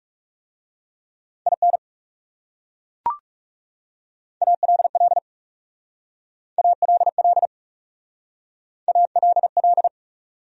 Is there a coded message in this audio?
Yes: Morse code from 1.5 to 9.9 s, heard as a keyed tone, about 40 wpm, 700 Hz. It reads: IN ALL ALL ALL